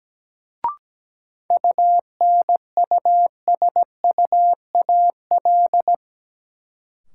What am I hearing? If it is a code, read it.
Morse code, heard as a keyed tone, about 17 wpm, 700 Hz. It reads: UNUSUAL
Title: Sentences from Top 1400 Words - New Word Drill - 17wpm